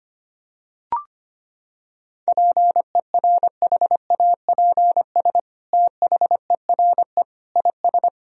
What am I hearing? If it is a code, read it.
Morse code, heard as a keyed tone, about 25 wpm, 700 Hz. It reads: PERHAPS THERE IS
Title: Sentences from Top 1000 Words - Review Part 4 - 25wpm